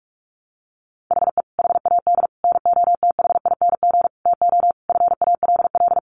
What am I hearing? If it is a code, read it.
Morse code, heard as a keyed tone, about 44 wpm, 700 Hz. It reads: HE HAD NOTHING TO FALL
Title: Sentences from Top 1100 Words 45wpm